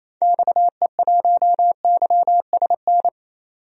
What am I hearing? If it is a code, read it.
Morse code, heard as a keyed tone, about 28 wpm, 700 Hz. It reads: XE1YSN
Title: DXCC-100 Call Signs - 6 Characters - 28wpm